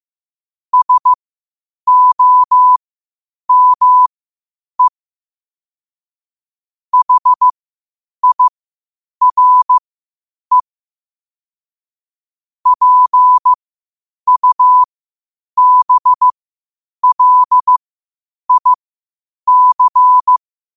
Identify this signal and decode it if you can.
Morse code, heard as a keyed tone, about 15 wpm, spaced as for 8 wpm, 1000 Hz. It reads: SOME HIRE PUBLIC
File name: qotdfiles / 8wpm